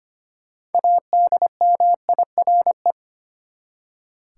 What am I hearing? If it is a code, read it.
Morse code, heard as a keyed tone, about 25 wpm, 700 Hz. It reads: ADMIRE
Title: Six Letter Words - 25wpm